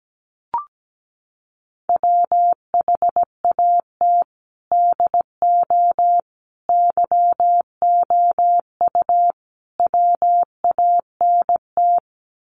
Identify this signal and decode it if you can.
Morse code, heard as a keyed tone, about 17 wpm, 700 Hz. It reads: WHAT DO YOU WANT
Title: Sets of 4 Words 17wpm